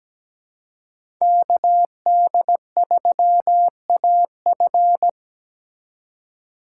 Morse code, heard as a keyed tone, about 20 wpm, 700 Hz. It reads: KD3AF
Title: Speed Racing - USA 2x2 Call Signs 17wpm